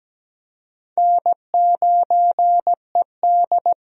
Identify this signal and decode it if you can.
Morse code, heard as a keyed tone, about 17 wpm, 700 Hz. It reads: N9ED
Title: USA Call Signs - 4 Characters - 17wpm